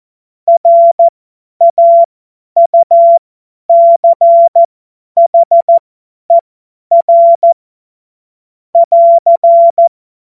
Morse code, heard as a keyed tone, about 14 wpm, spaced as for 10 wpm, 680 Hz. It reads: RAUCHER <AR>
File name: Morsekurs / Witze / Witz 09 April 2024 50 F70